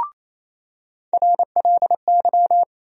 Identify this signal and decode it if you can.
Morse code, heard as a keyed tone, about 28 wpm, 700 Hz. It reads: RLY
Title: USA Call Sign Postfixes - 3 Letters 28wpm